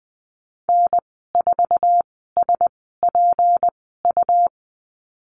Morse code, heard as a keyed tone, about 20 wpm, spaced as for 15 wpm, 700 Hz. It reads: N4SPU
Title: 219 Lesson 35g - Number 0 - Call Signs 20wpm